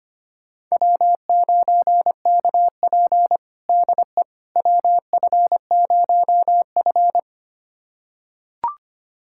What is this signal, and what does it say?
Morse code, heard as a keyed tone, about 25 wpm, 700 Hz. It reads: W9KP DE WF0F
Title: USA Callsign DE Callsign - 4 Characters - 25wpm